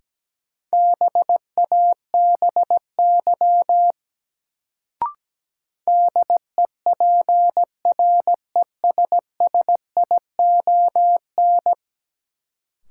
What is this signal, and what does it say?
Morse code, heard as a keyed tone, about 17 wpm, 700 Hz. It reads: BABY DEPRESSION
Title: Sentences from Top 1300 Words - New Word Drill - 17wpm